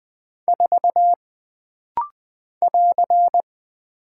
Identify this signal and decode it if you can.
Morse code, heard as a keyed tone, about 20 wpm, 700 Hz. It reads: 4 <AR>
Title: Single Character (Letter:Number:Punctuation:Prosign) - Rapid-Fire - 20wpm